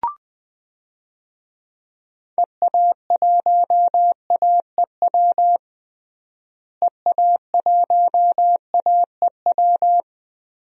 Morse code, heard as a keyed tone, about 21 wpm, 700 Hz. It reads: EA1AEW EA1AEW
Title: DXCC-100 Call Signs - 6 Characters - Repeated 3x - 20wpm